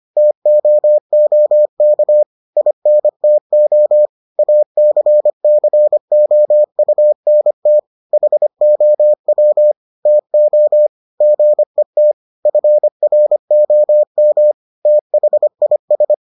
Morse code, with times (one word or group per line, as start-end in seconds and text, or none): 0.2-2.2 TOOK
2.6-4.1 INTO
4.4-7.8 ACCOUNT
8.1-9.7 HOW
10.1-10.9 TO
11.2-12.1 GET
12.5-14.5 FROM
14.9-16.1 THIS